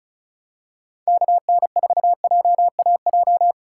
Morse code, heard as a keyed tone, about 36 wpm, 700 Hz. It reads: KN4JAJ
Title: USA Call Signs - 6 Characters - 2x Character Spacing - 35wpm